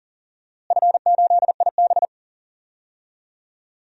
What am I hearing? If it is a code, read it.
Morse code, heard as a keyed tone, about 40 wpm, 700 Hz. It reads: F8IB
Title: DXCC-100 Call Signs - 4 Characters - Repeated 3x - 40wpm